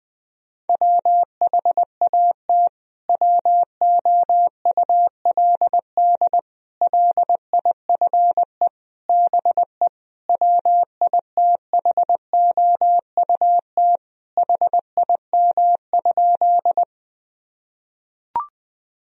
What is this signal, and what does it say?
Morse code, heard as a keyed tone, about 21 wpm, 700 Hz. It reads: WHAT WOULD LIFE BE WITHOUT HIM?
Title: Sentences from Top 1000 Words - Review Part 3 - 20wpm